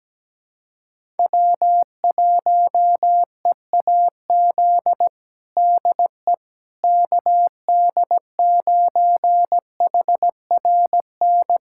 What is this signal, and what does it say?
Morse code, heard as a keyed tone, about 17 wpm, 700 Hz. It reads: W1EAZ DE KD9HRN
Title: USA Callsign DE Callsign 17wpm